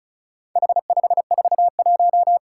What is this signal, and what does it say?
Morse code, heard as a keyed tone, about 34 wpm, 700 Hz. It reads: H541
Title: Four Challenging Letter-Number Combinations 35wpm